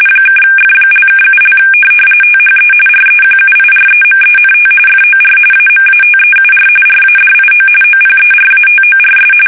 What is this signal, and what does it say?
RTTY, about 45.45 baud. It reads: (F)(7) THE REQUIREMENT FOR AMATEUR RADIO OPERATORS
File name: fsk850 / teletype-RTTY